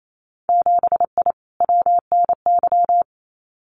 Morse code, heard as a keyed tone, about 28 wpm, 700 Hz. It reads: 7I WNY
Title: Winter Field Day Exchanges Encore 28wpm